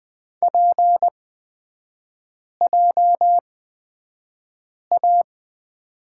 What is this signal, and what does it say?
Morse code, heard as a keyed tone, about 20 wpm, spaced as for 5 wpm, 700 Hz. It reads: PJA
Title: 250 Lesson 40e - BK - Three Characters 20wpm